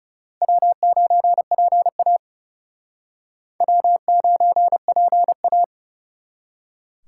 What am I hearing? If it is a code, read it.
Morse code, heard as a keyed tone, about 38 wpm, 700 Hz. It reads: W9PA W9PA